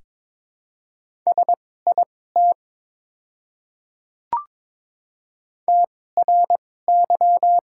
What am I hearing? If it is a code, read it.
Morse code, heard as a keyed tone, about 22 wpm, spaced as for 16 wpm, 690 Hz. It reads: SIT TRY
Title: 2x Character Spacing - Top 500 Words - 22wpm